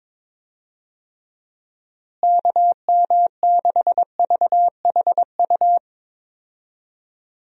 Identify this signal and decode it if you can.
Morse code, heard as a keyed tone, about 22 wpm, 700 Hz. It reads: KM6VHU
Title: USA Call Signs - 6 Characters - Repeated 3x - 22wpm